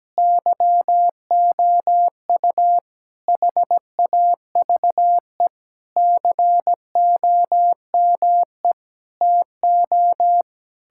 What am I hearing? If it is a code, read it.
Morse code, heard as a keyed tone, about 17 wpm, 700 Hz. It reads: YOU HAVE COME TO
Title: Sentences from Top 1600 Words 17wpm